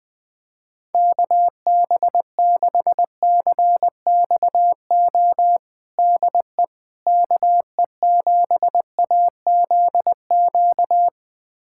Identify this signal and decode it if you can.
Morse code, heard as a keyed tone, about 20 wpm, 700 Hz. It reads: KB6CXO DE KE7AZQ
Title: USA Callsign DE Callsign - 6 Characters - 20wpm